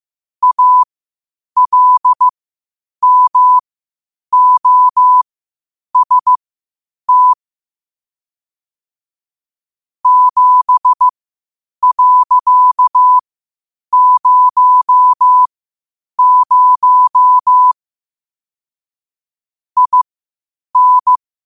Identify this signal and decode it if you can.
Morse code, heard as a keyed tone, about 15 wpm, spaced as for 8 wpm, 1000 Hz. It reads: ALMOST 7.00 IN
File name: qotdfiles / 8wpm